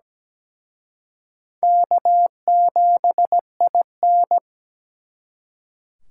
Morse code, heard as a keyed tone, about 20 wpm, 700 Hz. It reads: K7IN
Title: Speed Racing - USA 1x2 Call Signs 17wpm